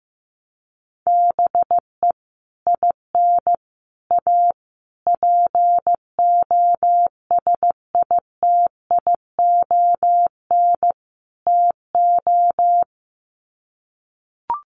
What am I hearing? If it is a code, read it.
Morse code, heard as a keyed tone, about 14 wpm, 700 Hz. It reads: BE IN A POSITION TO
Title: Sets of 5 Words Encore2 15wpm